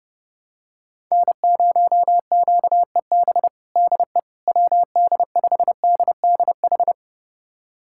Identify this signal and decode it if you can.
Morse code, heard as a keyed tone, about 30 wpm, 700 Hz. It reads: N0QEB DE WD5DDH
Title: USA Callsign DE Callsign 30wpm